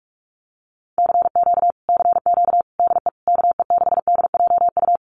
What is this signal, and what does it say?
Morse code, heard as a keyed tone, about 46 wpm, 700 Hz. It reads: CQ CQ DE KE6DJU
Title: Common QSO Phrases Encore 45wpm